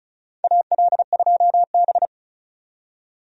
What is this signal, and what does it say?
Morse code, heard as a keyed tone, about 36 wpm, 700 Hz. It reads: AL2B
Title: USA Call Signs - 4 Characters - 35wpm